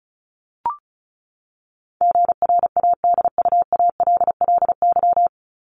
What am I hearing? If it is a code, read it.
Morse code, heard as a keyed tone, about 34 wpm, 700 Hz. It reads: GRADUALLY
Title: LY Suffix - Top 100 Words with Base Words 35wpm